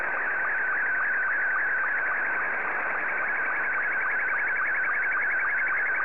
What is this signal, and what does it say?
SSTV (slow-scan television), Scottie 2.